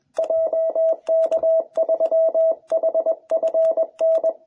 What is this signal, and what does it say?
Morse code, heard as a keyed tone, about 21 wpm, 660 Hz. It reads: JK3HFN